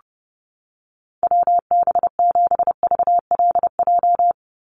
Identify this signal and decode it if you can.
Morse code, heard as a keyed tone, about 30 wpm, 700 Hz. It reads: WB7VLJ
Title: USA Call Signs 30wpm